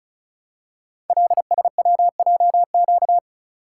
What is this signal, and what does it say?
Morse code, heard as a keyed tone, about 36 wpm, 700 Hz. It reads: LSWJQ
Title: Code Groups - 5 Letter:Numbers - 35wpm